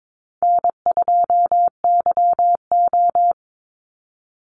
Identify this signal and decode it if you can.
Morse code, heard as a keyed tone, about 22 wpm, 700 Hz. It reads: N2YO
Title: Easy Call Signs Repeated 3x - 22wpm